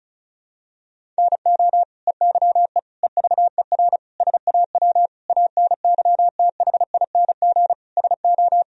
Morse code, heard as a keyed tone, about 36 wpm, 700 Hz. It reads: NO EYE EVER SAW ANYTHING SO
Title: Sentences from Top 1000 Words - Review Part 1 - 35wpm